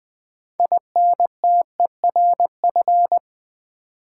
Morse code, heard as a keyed tone, about 20 wpm, 700 Hz. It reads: INTERF
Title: Build A Word - 3 Letters At A Time - 9-Letter Words - 20wpm